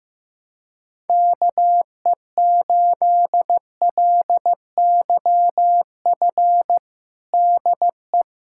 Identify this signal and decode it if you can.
Morse code, heard as a keyed tone, about 15 wpm, 700 Hz. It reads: KE8LYF DE
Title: USA Callsign DE Callsign 15wpm